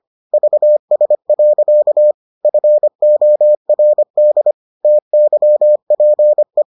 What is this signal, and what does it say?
Morse code, heard as a keyed tone, about 25 wpm, 600 Hz. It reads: VS. FORD TYPE